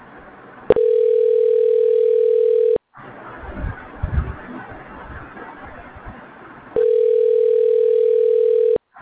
Ringback tone (North America), 440 + 480 Hz.
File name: recordings / REeb6667b34fd0a1f483b6cb80bea5ebc4